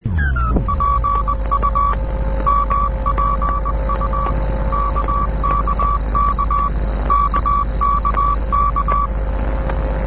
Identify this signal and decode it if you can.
Morse code, heard as a keyed tone, about 20 wpm, 1140 Hz. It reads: PU MPU KKK KKK
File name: raf malta 'mpu' cw (1974)